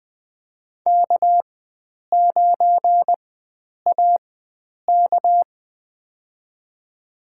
Morse code, heard as a keyed tone, about 20 wpm, spaced as for 9 wpm, 700 Hz. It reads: K9AK
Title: USA Call Signs - 4 Characters - 4x Character Spacing - 20wpm